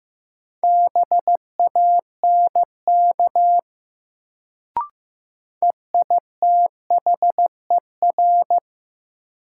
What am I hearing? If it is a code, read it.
Morse code, heard as a keyed tone, about 15 wpm, 700 Hz. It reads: BANK EITHER
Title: Sets of 5 Words - New Word Drill - 15wpm